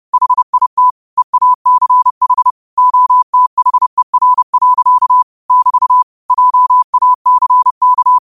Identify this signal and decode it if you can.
Morse code, heard as a keyed tone, about 30 wpm, 1000 Hz. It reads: HIT EACH OTHER. <BT> JACK